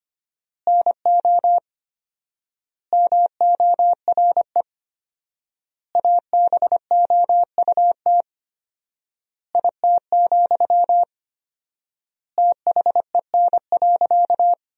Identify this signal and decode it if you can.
Morse code, heard as a keyed tone, about 25 wpm, 700 Hz. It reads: NO MORE ABOUT IT, THEN.
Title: 4x Word-Spacing - Sentences from Top 200 Words - 25wpm